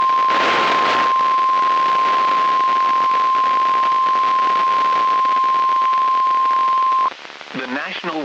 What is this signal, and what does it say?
Emergency Alert System (SAME) tones (the 1050 Hz attention tone), with speech. It is heard from 0.9 to 7.1 s.
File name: eas messages / EAS WXM95 Towanda